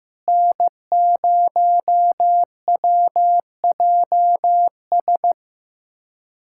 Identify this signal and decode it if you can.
Morse code, heard as a keyed tone, about 15 wpm, 700 Hz. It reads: N0WJS
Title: USA Callsign DE Callsign - 5 Characters - 15wpm